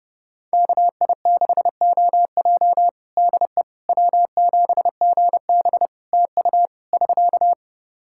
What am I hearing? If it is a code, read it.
Morse code, heard as a keyed tone, about 32 wpm, 700 Hz. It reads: KI6OJ DE W7GB TU <SK>
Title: Practice QSO - Exchange 4 - Ending - 30wpm